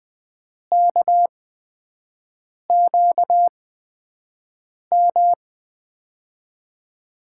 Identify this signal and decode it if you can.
Morse code, heard as a keyed tone, about 20 wpm, spaced as for 5 wpm, 700 Hz. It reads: KQM
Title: USA Call Signs - 8x Character Spacing - 20wpm